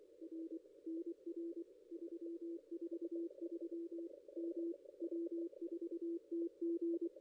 Morse code, heard as a keyed tone, about 24 wpm, 340 Hz, with radio noise. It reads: R NR 343 M W4TG
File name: contests / ss-cw-k5tr-op-n5rz -1600